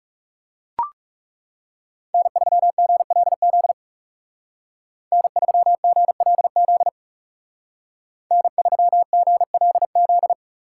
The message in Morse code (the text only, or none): N3GLZ N3GLZ N3GLZ